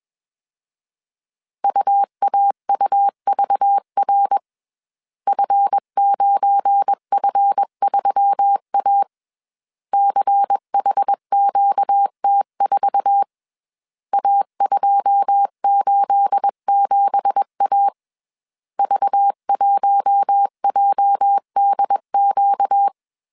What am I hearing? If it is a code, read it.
Morse code, heard as a keyed tone, about 21 wpm, 800 Hz. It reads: UAUVR F9F3A CHQT4 A287A V1JDQ